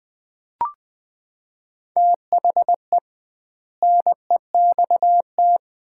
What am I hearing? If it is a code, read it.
Morse code, heard as a keyed tone, about 20 wpm, 700 Hz. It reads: THE NEXT
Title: 2x Word-Spacing - Sets of 3 Words - 20wpm